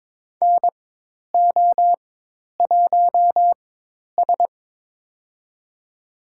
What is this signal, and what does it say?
Morse code, heard as a keyed tone, about 22 wpm, spaced as for 10 wpm, 700 Hz. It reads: NO1S
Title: USA Call Signs - 4 Characters - 4x Character Spacing - 22wpm